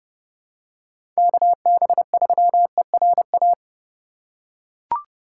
Morse code, heard as a keyed tone, about 30 wpm, 700 Hz. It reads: KB3ERA
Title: USA Call Signs - 6 Characters - 4x Character Spacing - 30wpm